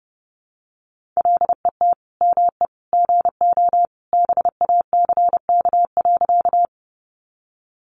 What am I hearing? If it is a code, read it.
Morse code, heard as a keyed tone, about 30 wpm, 700 Hz. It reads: LET ME GO BACK.
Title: Sentences from Top 300 Words 30wpm